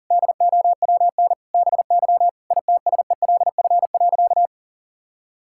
Morse code, heard as a keyed tone, about 40 wpm, 700 Hz. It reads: DOWN BY ITSELF.
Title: Sentences from Top 1000 Words - Review Part 5 - 40wpm